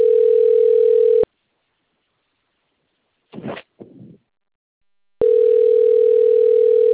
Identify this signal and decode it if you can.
Ringback tone (North America), 440 + 480 Hz.
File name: rg-1600-unknown- - - .96083